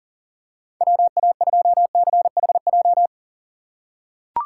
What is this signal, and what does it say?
Morse code, heard as a keyed tone, about 40 wpm, 700 Hz. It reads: WA2CHJ
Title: USA Call Signs - 6 Characters - 6x Character Spacing - 40wpm